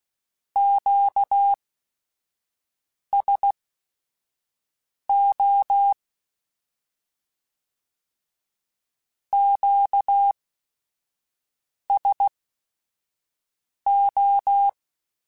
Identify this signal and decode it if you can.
Morse code, heard as a keyed tone, about 16 wpm, spaced as for 5 wpm, 800 Hz. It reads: QSO QSO
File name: Q Calls / Q Calls-mp3s / Q Calls Twice @5